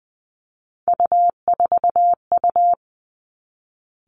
Morse code, heard as a keyed tone, about 20 wpm, 700 Hz. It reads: U4U
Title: Three Challenging Letter-Number Combinations 20wpm